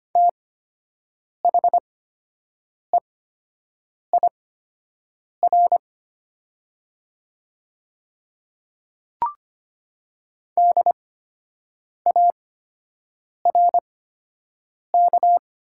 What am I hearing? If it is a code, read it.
Morse code, heard as a keyed tone, about 25 wpm, spaced as for 7 wpm, 700 Hz. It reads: THEIR DARK
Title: 8x Character Spacing - Top 500 Words - 25wpm